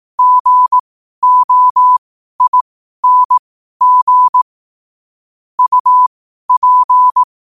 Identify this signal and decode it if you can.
Morse code, heard as a keyed tone, about 18 wpm, spaced as for 12 wpm, 1000 Hz. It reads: GOING UP
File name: qotdfiles / 12wpm